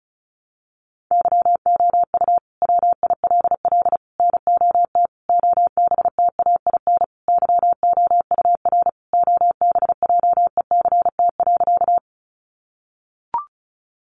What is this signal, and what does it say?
Morse code, heard as a keyed tone, about 34 wpm, 700 Hz. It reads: YOU WILL NOT OBTAIN YOUR OBJECT.
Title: Sentences from Top 1300 Words 35wpm